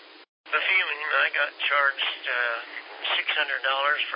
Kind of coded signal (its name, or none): Emergency Alert System (SAME) tones